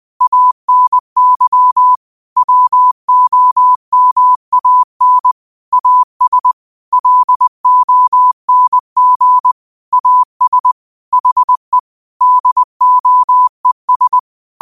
Morse code, heard as a keyed tone, about 20 wpm, 1000 Hz. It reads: ANY WOMAN AS LONG AS HE DOES
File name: qotdfiles / 20wpm